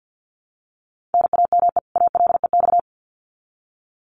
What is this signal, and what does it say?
Morse code, heard as a keyed tone, about 50 wpm, 700 Hz. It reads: NAME ALEX
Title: Common QSO Phrases 50wpm